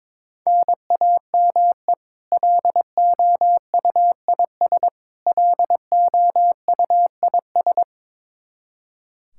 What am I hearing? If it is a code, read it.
Morse code, heard as a keyed tone, about 22 wpm, 700 Hz. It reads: NAME LOUIS LOUIS
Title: K1USN SST QSO Practice 22wpm